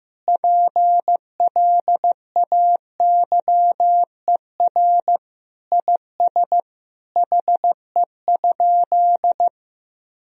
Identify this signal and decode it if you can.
Morse code, heard as a keyed tone, about 16 wpm, 700 Hz. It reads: PLAYER IS HE?
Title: Sentences from Top 1000 Words - Review Part 4 - 15wpm